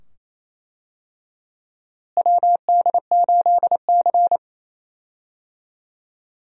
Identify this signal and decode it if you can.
Morse code, heard as a keyed tone, about 28 wpm, 700 Hz. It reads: WD8C
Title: USA Call Signs - 4 Characters - Repeated 2x - 28wpm